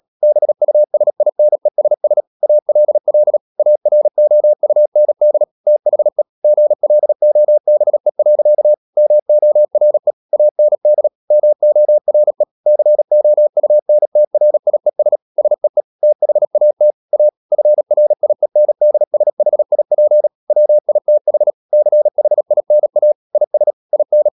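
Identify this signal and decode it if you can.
Morse code, heard as a keyed tone, about 37 wpm, 600 Hz. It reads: BUSINESS ALL AROUND THE GLOBE. MORE AND MORE COUNTRIES SEE THAT A FRIENDSHIP WITH CHINA IS IN
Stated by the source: Morse code practice at 35–39 wpm sending English text of Ken Follett's novel Never, from CW med Gnister.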